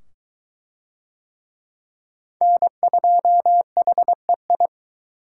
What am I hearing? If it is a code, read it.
Morse code, heard as a keyed tone, about 22 wpm, 700 Hz. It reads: N2HEI